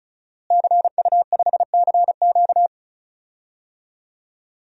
Morse code, heard as a keyed tone, about 36 wpm, 700 Hz. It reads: CU5CQ
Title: DXCC-100 Call Signs - 5 Characters - Repeated 3x - 35wpm